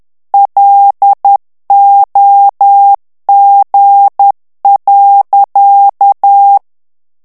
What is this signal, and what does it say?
Morse code, heard as a keyed tone, about 10 wpm, 800 Hz. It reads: LOG.
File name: ITALIEN texte clair 2 vitesse 10 mots minute